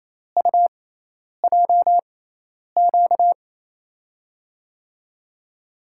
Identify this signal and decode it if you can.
Morse code, heard as a keyed tone, about 28 wpm, spaced as for 10 wpm, 700 Hz. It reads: UJQ
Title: USA Call Signs - 5 Characters - 6x Character Spacing - 28wpm